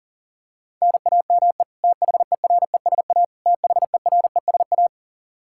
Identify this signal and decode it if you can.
Morse code, heard as a keyed tone, about 40 wpm, 700 Hz. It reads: NAME THERESA THERESA
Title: Practice QSO - Exchange 1 - Beginning - 40wpm